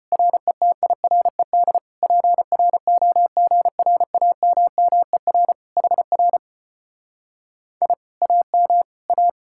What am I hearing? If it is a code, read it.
Morse code, heard as a keyed tone, about 33 wpm, 700 Hz. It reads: RETIRED PROGRAMMER HR I AM A